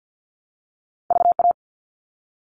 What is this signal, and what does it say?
Morse code, heard as a keyed tone, about 50 wpm, 700 Hz. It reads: VA
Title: ARRL Sections 50wpm